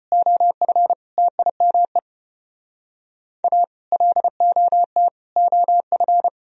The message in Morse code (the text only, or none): OF TIME A LOT OF